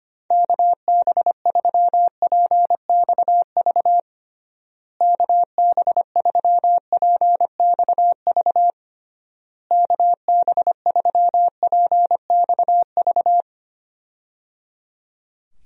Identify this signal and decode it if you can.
Morse code, heard as a keyed tone, about 25 wpm, 700 Hz. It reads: KB3PXV KB3PXV KB3PXV
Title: USA Call Signs - 6 Characters - Repeated 3x - 25wpm